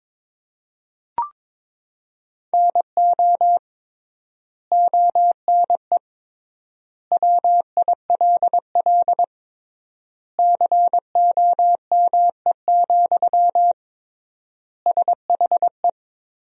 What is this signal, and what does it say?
Morse code, heard as a keyed tone, about 22 wpm, 700 Hz. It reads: NO ONE WILL COME, SHE
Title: 3x Word-Spacing - Sentences from Top 100 Words - 22wpm